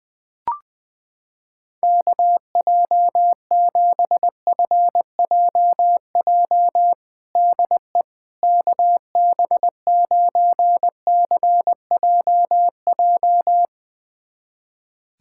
Morse code, heard as a keyed tone, about 20 wpm, 700 Hz. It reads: KJ7FJJ DE KB9CJJ
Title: USA Callsign DE Callsign - 6 Characters - 20wpm